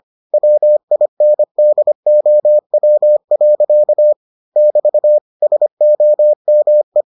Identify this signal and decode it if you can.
Morse code, heard as a keyed tone, about 25 wpm, 600 Hz. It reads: WINDOW. <BT> SOME